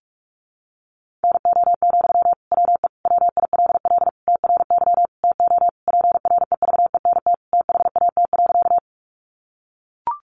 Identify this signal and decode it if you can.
Morse code, heard as a keyed tone, about 44 wpm, 700 Hz. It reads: NO, WE WILL TRY TO PREVENT THAT.
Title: Sentences from Top 1100 Words 45wpm